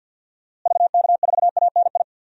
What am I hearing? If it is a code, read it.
Morse code, heard as a keyed tone, about 53 wpm, 700 Hz. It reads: VK4ANI
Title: DXCC-100 Call Signs - 6 Characters - Repeated 2x - 50wpm